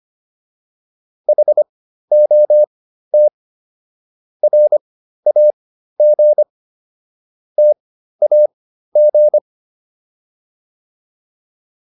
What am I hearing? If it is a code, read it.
Morse code, heard as a keyed tone, about 25 wpm, spaced as for 13 wpm, 600 Hz. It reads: HOT RAG TAG